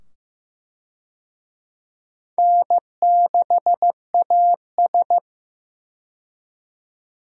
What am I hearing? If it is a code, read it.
Morse code, heard as a keyed tone, about 15 wpm, 700 Hz. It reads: N6AS
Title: USA Call Signs - 4 Characters - 15wpm